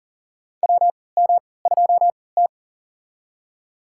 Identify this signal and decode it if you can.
Morse code, heard as a keyed tone, about 40 wpm, spaced as for 23 wpm, 700 Hz. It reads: WM2T
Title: USA Call Signs - 4 Characters - 3x Character Spacing - 40wpm